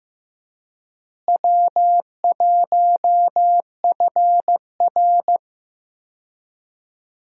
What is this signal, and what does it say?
Morse code, heard as a keyed tone, about 15 wpm, 700 Hz. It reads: W1FR
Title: USA Call Signs - 4 Characters - Repeated 3x - 15wpm